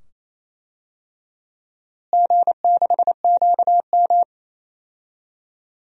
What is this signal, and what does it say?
Morse code, heard as a keyed tone, about 28 wpm, 700 Hz. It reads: G6QM
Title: DXCC-100 Call Signs - 4 Characters - Repeated 2x - 28wpm